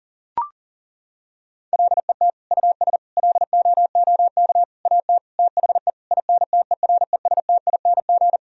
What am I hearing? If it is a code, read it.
Morse code, heard as a keyed tone, about 40 wpm, 700 Hz. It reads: LET US LOOK AT THE INTERESTING
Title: Sentences from Top 1000 Words - Review Part 2 - 40wpm